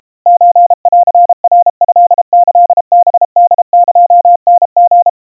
Morse code (Japanese Wabun code), heard as a keyed tone, about 32 wpm, 700 Hz. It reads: ソンナトキハホエタリ
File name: kawa / Morse sample / kamyu 120 cpm 700 Hz Crispy